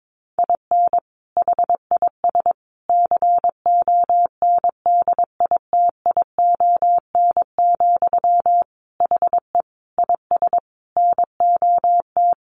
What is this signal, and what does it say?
Morse code, heard as a keyed tone, about 22 wpm, 700 Hz. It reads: IN HIS CONDITION, HE IS NOT
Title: Sentences from Top 1200 Words 22wpm